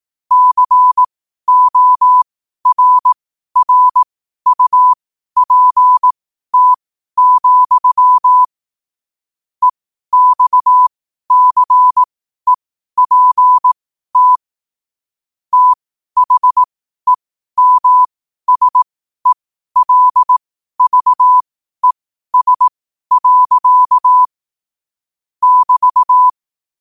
Morse code, heard as a keyed tone, about 18 wpm, spaced as for 12 wpm, 1000 Hz. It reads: CORRUPT, EXCEPT THEMSELVES. <BT>